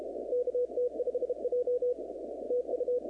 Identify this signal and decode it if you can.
Morse code, heard as a keyed tone, about 33 wpm, 490 Hz, with radio noise. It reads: KT5J TU
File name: contests / 2019-cw-wpx-kt5j-k5pi- -2330 0-19